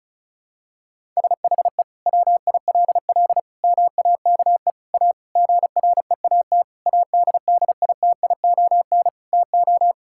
Morse code, heard as a keyed tone, about 34 wpm, 700 Hz. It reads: SHE WILL MAKE A GREAT ADDITION TO
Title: Sentences from Top 1000 Words - Review Part 6 - 35wpm